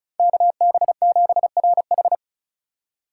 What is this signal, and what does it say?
Morse code, heard as a keyed tone, about 36 wpm, 700 Hz. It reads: KB7RH
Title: USA Call Signs - 5 Characters - 2x Character Spacing - 35wpm